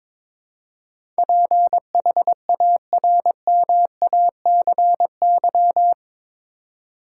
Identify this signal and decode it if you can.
Morse code, heard as a keyed tone, about 22 wpm, 700 Hz. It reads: PHARMACY
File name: practice / 22wpm / Eight Letter Words - 22wpm